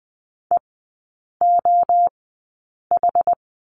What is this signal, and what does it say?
Morse code, heard as a keyed tone, about 20 wpm, spaced as for 8 wpm, 700 Hz. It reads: EOH
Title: 074 Lesson 13e - Letter L - Three Characters 20wpm